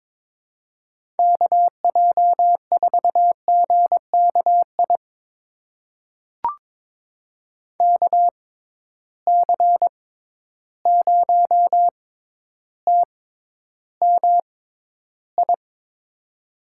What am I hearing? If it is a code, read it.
Morse code, heard as a keyed tone, about 22 wpm, spaced as for 11 wpm, 700 Hz. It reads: KJ4GKI KC0TMI
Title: USA Call Signs - 6 Characters - 6x Character Spacing - 22wpm